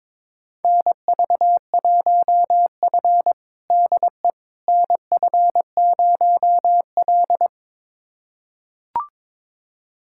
Morse code, heard as a keyed tone, about 21 wpm, 700 Hz. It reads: NV1F DE NF0L
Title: USA Callsign DE Callsign - 4 Characters - 22wpm